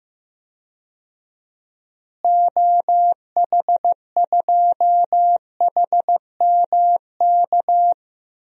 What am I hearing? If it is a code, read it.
Morse code, heard as a keyed tone, about 15 wpm, 700 Hz. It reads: OH2HMK
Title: DXCC-100 Call Signs - 6 Characters - Repeated 3x - 15wpm